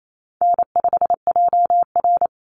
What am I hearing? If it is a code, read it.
Morse code, heard as a keyed tone, about 28 wpm, 700 Hz. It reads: N5JR
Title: Call Signs Repeated 2x - 28wpm